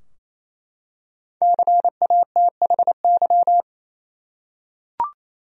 Morse code, heard as a keyed tone, about 28 wpm, 700 Hz. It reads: CATHY
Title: USA Common Names - 28wpm